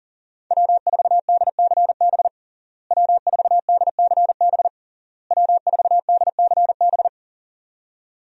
Morse code, heard as a keyed tone, about 40 wpm, 700 Hz. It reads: W4DCB W4DCB W4DCB
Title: USA Call Signs - 5 Characters - Repeated 3x - 40wpm